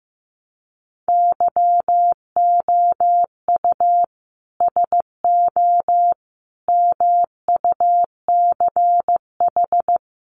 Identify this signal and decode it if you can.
Morse code, heard as a keyed tone, about 15 wpm, 700 Hz. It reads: YOU SO MUCH
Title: Sets of 5 Words Encore2 15wpm